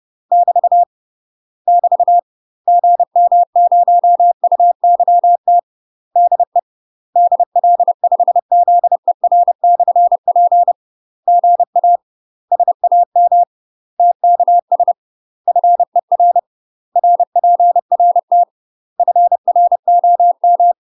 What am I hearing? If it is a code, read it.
Morse code, heard as a keyed tone, about 30 wpm, 700 Hz. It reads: <BT> <BT> GM0UYT DE DL5ZER/P GA SAM TKS FER RPRT FROM
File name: UK MASTER QSOs TRAINING / 30WPM / UK MASTER QSOs 0004